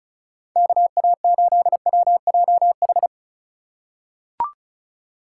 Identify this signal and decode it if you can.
Morse code, heard as a keyed tone, about 35 wpm, 700 Hz. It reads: KA8WJH
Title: USA Call Signs - 6 Characters - 3x Character Spacing - 35wpm